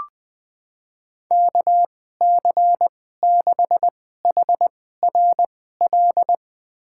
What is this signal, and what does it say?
Morse code, heard as a keyed tone, about 20 wpm, spaced as for 15 wpm, 700 Hz. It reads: KC6HRL
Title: USA Call Signs - 2x Character Spacing - 20wpm